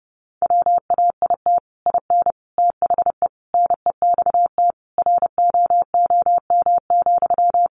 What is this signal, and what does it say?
Morse code, heard as a keyed tone, about 30 wpm, 700 Hz. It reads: WAIT IN THE NEXT ROOM,
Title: Sentences from Top 1500 Words 30wpm